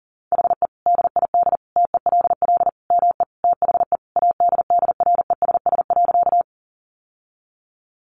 Morse code, heard as a keyed tone, about 40 wpm, 700 Hz. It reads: HE DID TELL ME THE ADDRESS.
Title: Sentences from Top 1000 Words 40wpm